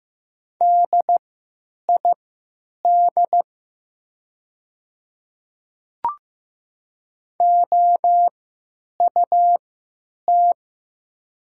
Morse code, heard as a keyed tone, about 15 wpm, spaced as for 8 wpm, 700 Hz. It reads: DID OUT
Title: 3x Character Spacing - Top 100 Words - 15wpm